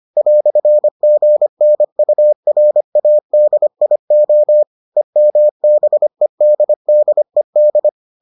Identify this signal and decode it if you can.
Morse code, heard as a keyed tone, about 25 wpm, 600 Hz. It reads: "GNURADIO EMBEDDED